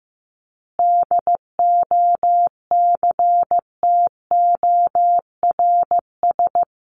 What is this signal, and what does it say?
Morse code, heard as a keyed tone, about 15 wpm, 700 Hz. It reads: DOCTORS
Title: Easy Phrases 15wpm